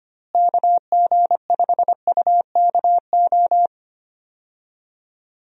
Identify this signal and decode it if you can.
Morse code, heard as a keyed tone, about 25 wpm, 700 Hz. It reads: KG5UKO
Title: USA Call Signs - 6 Characters - Repeated 3x - 25wpm